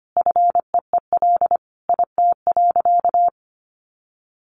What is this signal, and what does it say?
Morse code, heard as a keyed tone, about 25 wpm, 700 Hz. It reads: FEEL IT.
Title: Sentences from Top 400 Words 25wpm